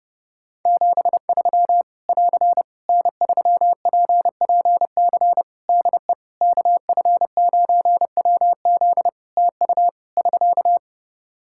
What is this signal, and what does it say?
Morse code, heard as a keyed tone, about 30 wpm, 700 Hz. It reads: 73 <AR> N3PPC DE KF9WZ TU <SK>
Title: K1USN SST QSO Practice 30wpm